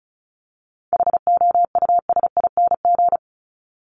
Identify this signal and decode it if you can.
Morse code, heard as a keyed tone, about 34 wpm, 700 Hz. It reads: HOUSING
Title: Top 1001-1250 Words 35wpm